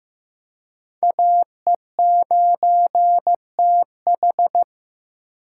Morse code, heard as a keyed tone, about 15 wpm, 700 Hz. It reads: AE9TH
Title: USA Call Signs - 5 Characters - 3x Character Spacing - 15wpm